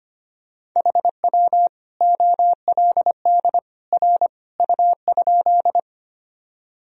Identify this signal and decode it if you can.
Morse code, heard as a keyed tone, about 25 wpm, 700 Hz. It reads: HW OLD R U?
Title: Sentences from Top 200 Words and Abbreviations 25wpm